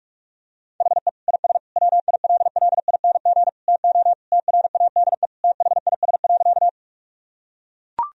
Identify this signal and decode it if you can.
Morse code, heard as a keyed tone, about 41 wpm, 700 Hz. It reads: HE IS WILLING TO TRADE THIS.
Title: Sentences from Top 1000 Words - Review Part 4 - 45wpm